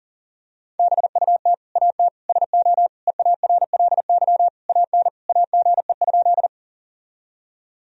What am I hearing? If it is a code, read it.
Morse code, heard as a keyed tone, about 40 wpm, 700 Hz. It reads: BUT AT SO EARLY AN AGE?
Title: Sentences from Top 1000 Words - Review Part 3 - 40wpm